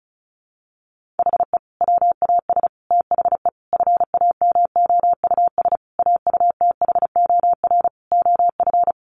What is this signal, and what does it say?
Morse code, heard as a keyed tone, about 34 wpm, 700 Hz. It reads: HE WAS THE FAMOUS AUTHOR OF
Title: Sentences from Top 1500 Words 35wpm